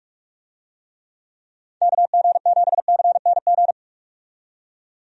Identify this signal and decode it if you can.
Morse code, heard as a keyed tone, about 46 wpm, 700 Hz. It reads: KG7CNG
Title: USA Call Signs - 6 Characters - 45wpm